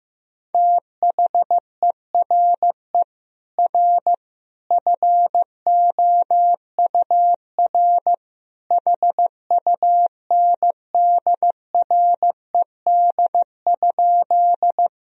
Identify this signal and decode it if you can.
Morse code, heard as a keyed tone, about 15 wpm, 700 Hz. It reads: THERE R FOUR HUNDRED?
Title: Sentences from Top 400 Words and Abbreviations 15wpm